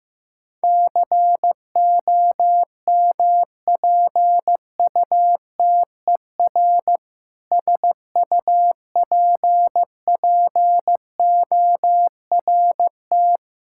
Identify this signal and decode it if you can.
Morse code, heard as a keyed tone, about 16 wpm, 700 Hz. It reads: COMPUTER SUPPORT
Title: Practice QSO - Exchange 3 - More Info - 15wpm